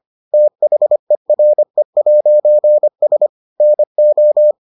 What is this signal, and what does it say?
Morse code, heard as a keyed tone, about 25 wpm, 600 Hz. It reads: THERE'S NO